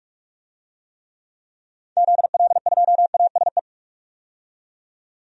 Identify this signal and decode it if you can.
Morse code, heard as a keyed tone, about 41 wpm, 700 Hz. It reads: ZL2ASE
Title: DXCC-100 Call Signs - 6 Characters - 45wpm